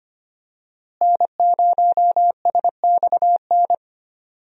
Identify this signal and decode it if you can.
Morse code, heard as a keyed tone, about 25 wpm, 700 Hz. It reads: N0SXN
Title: USA Call Signs - 5 Characters - Repeated 2x - 25wpm